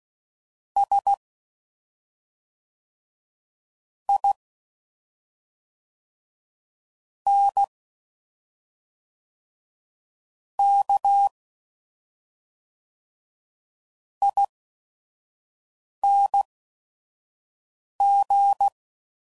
Morse code, heard as a keyed tone, about 16 wpm, spaced as for 5 wpm, 800 Hz. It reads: S I N K ING
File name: Suffixes / Special ING's / SlowToFaster ING's